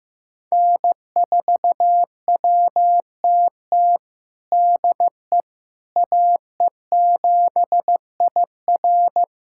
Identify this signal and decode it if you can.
Morse code, heard as a keyed tone, about 15 wpm, 700 Hz. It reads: N4WTT DE AE7IR
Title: USA Callsign DE Callsign - 5 Characters - 15wpm